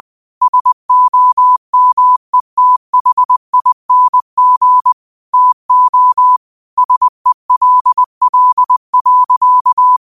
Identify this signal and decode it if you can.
Morse code, heard as a keyed tone, about 20 wpm, 1000 Hz. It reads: SOMETHING TO SELL.
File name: qotdfiles / 20wpm